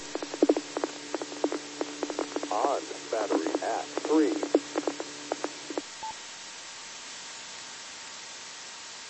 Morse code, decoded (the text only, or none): none